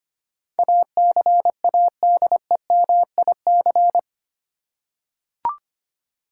Morse code, heard as a keyed tone, about 25 wpm, 700 Hz. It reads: ACADEMIC
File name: Eight Letter Words - 25wpm